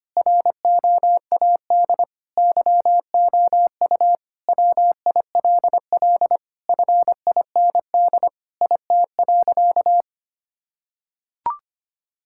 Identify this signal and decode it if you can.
Morse code, heard as a keyed tone, about 25 wpm, 700 Hz. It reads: ROAD YOU WILL FIND IT.
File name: practice / 25wpm / Sentences with 3-Word Phrases - 25wpm